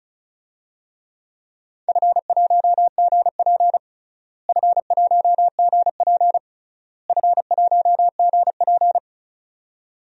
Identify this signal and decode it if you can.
Morse code, heard as a keyed tone, about 36 wpm, 700 Hz. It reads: F1GP F1GP F1GP
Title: DXCC-100 Call Signs - 4 Characters - Repeated 3x - 35wpm